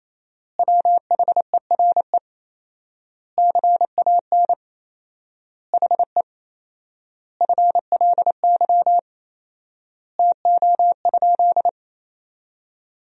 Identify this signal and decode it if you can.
Morse code, heard as a keyed tone, about 28 wpm, 700 Hz. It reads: WHERE CAN HE FLY TO?
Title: 4x Word-Spacing - Sentences from Top 500 Words - 28wpm